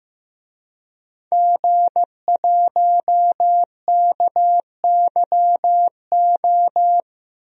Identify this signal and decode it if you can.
Morse code, heard as a keyed tone, about 15 wpm, 700 Hz. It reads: G1KYO
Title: DXCC-100 Call Signs - 5 Characters - 15wpm